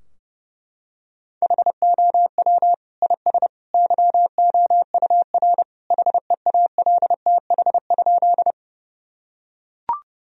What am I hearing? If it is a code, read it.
Morse code, heard as a keyed tone, about 32 wpm, 700 Hz. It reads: HOW IS YOUR HEALTH?
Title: Sentences from Top 1000 Words - Review Part 3 - 30wpm